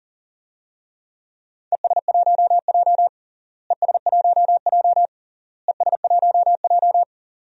Morse code, heard as a keyed tone, about 40 wpm, 700 Hz. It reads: ES1J ES1J ES1J
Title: DXCC-100 Call Signs - 4 Characters - Repeated 3x - 40wpm